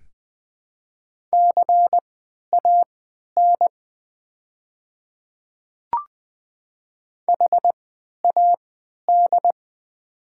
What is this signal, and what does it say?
Morse code, heard as a keyed tone, about 20 wpm, spaced as for 11 wpm, 700 Hz. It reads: CAN HAD
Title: 3x Character Spacing - Top 100 Words - 20wpm